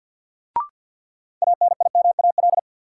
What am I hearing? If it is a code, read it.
Morse code, heard as a keyed tone, about 52 wpm, 700 Hz. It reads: ANIMAL
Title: Sentences from Top 1000 Words - Review Part 1 - New Word Drill - 50wpm